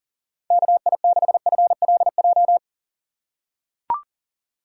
Morse code, heard as a keyed tone, about 40 wpm, 700 Hz. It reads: KI6FLJ
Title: USA Call Signs - 6 Characters - 4x Character Spacing - 40wpm